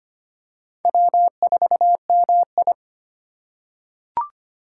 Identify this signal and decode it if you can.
Morse code, heard as a keyed tone, about 25 wpm, 700 Hz. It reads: W4MI
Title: USA Call Signs - 4 Characters - 2x Character Spacing - 25wpm